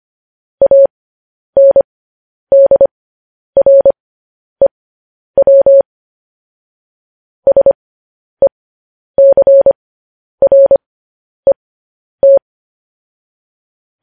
Morse code, heard as a keyed tone, about 25 wpm, spaced as for 10 wpm, 550 Hz. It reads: ANDREW SECRET
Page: news headlines in morse code at 10 wpm